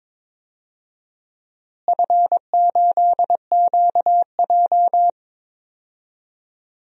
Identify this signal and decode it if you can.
Morse code, heard as a keyed tone, about 22 wpm, 700 Hz. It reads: F8QJ
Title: DXCC-100 Call Signs - 4 Characters - Repeated 3x - 22wpm